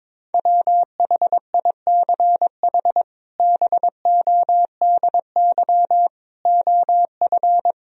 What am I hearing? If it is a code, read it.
Morse code, heard as a keyed tone, about 21 wpm, 700 Hz. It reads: WHICH BODY OF
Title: Sentences from Top 1000 Words - Review Part 2 - 22wpm